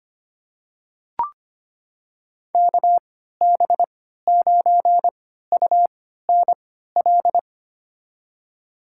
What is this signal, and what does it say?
Morse code, heard as a keyed tone, about 25 wpm, spaced as for 14 wpm, 700 Hz. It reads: KB9UNL
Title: USA Call Signs - 6 Characters - 3x Character Spacing - 25wpm